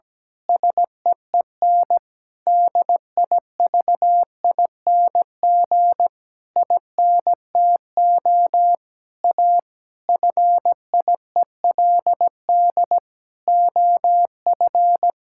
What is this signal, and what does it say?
Morse code, heard as a keyed tone, about 17 wpm, 700 Hz. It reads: SEEN DIVING INTO A FIELD OF
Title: Pangram Challenge v2 - 17wpm